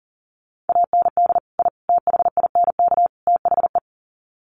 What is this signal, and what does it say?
Morse code, heard as a keyed tone, about 40 wpm, 700 Hz. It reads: AND I THINK THE
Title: Sets of 4 Words 40wpm